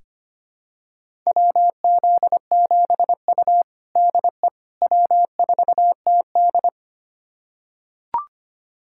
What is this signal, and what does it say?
Morse code, heard as a keyed tone, about 25 wpm, 700 Hz. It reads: WZ7U DE W4TD
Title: USA Callsign DE Callsign - 4 Characters - 25wpm